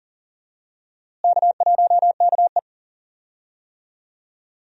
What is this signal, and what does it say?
Morse code, heard as a keyed tone, about 40 wpm, 700 Hz. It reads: K1KE
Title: USA Call Signs - 4 Characters - Repeated 3x - 40wpm